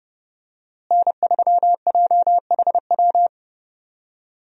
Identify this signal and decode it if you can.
Morse code, heard as a keyed tone, about 30 wpm, 700 Hz. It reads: N3JHW
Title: USA Call Signs - 5 Characters - Repeated 2x - 30wpm